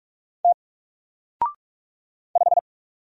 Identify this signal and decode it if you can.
Morse code, heard as a keyed tone, about 46 wpm, 700 Hz. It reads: T 5
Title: Single Letter-Number 45wpm